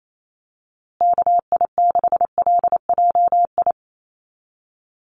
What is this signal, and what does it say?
Morse code, heard as a keyed tone, about 28 wpm, 700 Hz. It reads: KI6LJI
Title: USA Call Signs Repeated 2x 28wpm